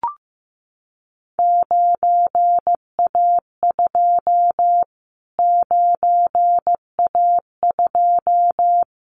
Morse code, heard as a keyed tone, about 15 wpm, 700 Hz. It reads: 9A2 9A2
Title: Callsign Prefixes for Top 25 DXCC - Repeated 3x - 15wpm